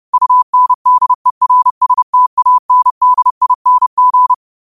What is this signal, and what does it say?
Morse code, heard as a keyed tone, about 30 wpm, 1000 Hz. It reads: UNDERSTANDING